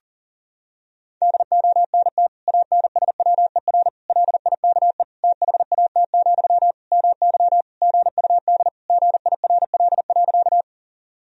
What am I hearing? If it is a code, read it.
Morse code, heard as a keyed tone, about 40 wpm, 700 Hz. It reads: DONT ANSWER LIKE THAT, MY GUD GIRL.
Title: Sentences from Top 400 Words and Abbreviations 40wpm